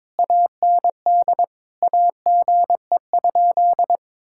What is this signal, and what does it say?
Morse code, heard as a keyed tone, about 22 wpm, 700 Hz. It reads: AND AGE?
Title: Sentences from Top 1600 Words 22wpm